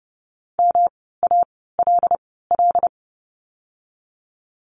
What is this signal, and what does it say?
Morse code, heard as a keyed tone, about 30 wpm, spaced as for 17 wpm, 700 Hz. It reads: MALL
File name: course / 30:10wpm / 108 Lesson 18f - Letter m - Words 30wpm